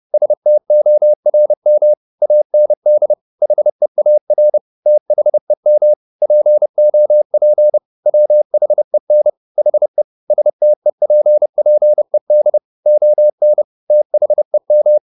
Morse code, heard as a keyed tone, about 30 wpm, 600 Hz. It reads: STORM AND HEAR THEM POP WHEN HE STEPPED ON THEM
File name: the old man and the sea / the old man and the sea 1 30wpm